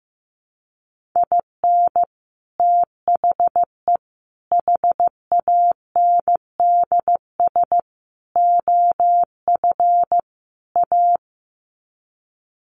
Morse code, heard as a keyed tone, about 15 wpm, 700 Hz. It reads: IN THE HANDS OF A
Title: Sets of 5 Words Encore 15wpm